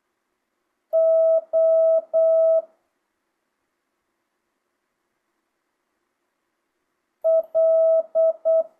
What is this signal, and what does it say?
Morse code, heard as a keyed tone, about 8 wpm, 650 Hz. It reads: O L
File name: jh8cbh / 2025cw